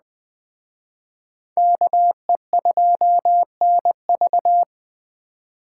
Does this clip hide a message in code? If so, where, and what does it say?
Yes: Morse code from 1.6 to 4.6 s, heard as a keyed tone, about 20 wpm, 700 Hz. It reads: KE2NV